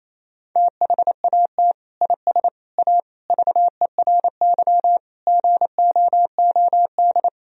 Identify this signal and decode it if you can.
Morse code, heard as a keyed tone, about 28 wpm, 700 Hz. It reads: THAT IS A VERY GOOD
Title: Sentences from Top 1000 Words - Review Part 4 - 28wpm